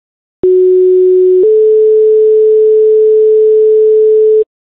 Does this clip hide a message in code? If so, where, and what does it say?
Yes: two-tone page from 0.4 to 4.4 s, heard as a one-second tone, then a three-second tone, about 370 then 430 Hz.